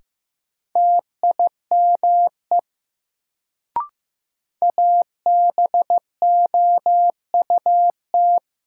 Morse code, heard as a keyed tone, about 15 wpm, 700 Hz. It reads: TIME ABOUT
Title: Sentences from Top 100 Words - New Word Drill - 15wpm